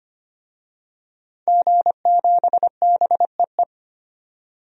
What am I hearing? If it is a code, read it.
Morse code, heard as a keyed tone, about 25 wpm, 700 Hz. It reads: G7BEE
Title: DXCC-100 Call Signs - 5 Characters - 25wpm